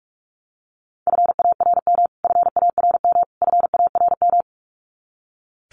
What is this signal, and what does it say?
Morse code, heard as a keyed tone, about 44 wpm, 700 Hz. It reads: FARM FARM FARM
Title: Top 500 Words Repeated 45wpm